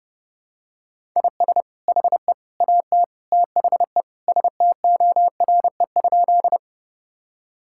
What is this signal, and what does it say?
Morse code, heard as a keyed tone, about 32 wpm, 700 Hz. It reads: IS HE AT THE STORE?
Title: Sentences from Top 1000 Words - Review Part 5 - 30wpm